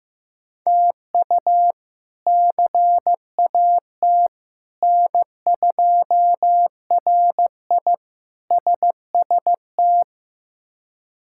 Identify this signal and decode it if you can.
Morse code, heard as a keyed tone, about 15 wpm, 700 Hz. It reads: TU CAT N2RI SST
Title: K1USN SST QSO Practice - 15wpm